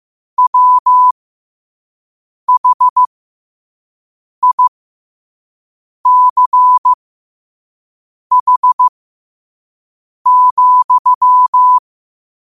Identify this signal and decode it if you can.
Morse code, heard as a keyed tone, about 15 wpm, spaced as for 5 wpm, 1000 Hz. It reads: WHICH,
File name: qotdfiles / 5wpm